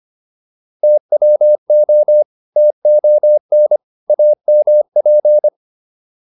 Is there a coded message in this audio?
Yes: Morse code from 0.8 to 5.5 s, heard as a keyed tone, about 25 wpm, 600 Hz. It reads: TWO TON AMP